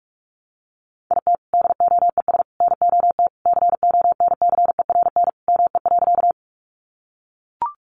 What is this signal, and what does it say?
Morse code, heard as a keyed tone, about 44 wpm, 700 Hz. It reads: IT DOES NOT CONCERN ME.
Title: Sentences from Top 1000 Words 45wpm